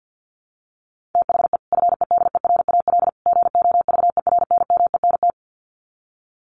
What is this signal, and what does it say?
Morse code, heard as a keyed tone, about 50 wpm, 700 Hz. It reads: THE FEDERAL GOVERNMENT
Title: Sets of 3 Words 50wpm